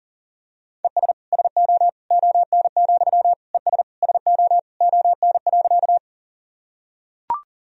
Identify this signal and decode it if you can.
Morse code, heard as a keyed tone, about 40 wpm, 700 Hz. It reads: ES SO ON, ES SO ON.
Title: Sentences from Top 100 Words and Abbreviations 40wpm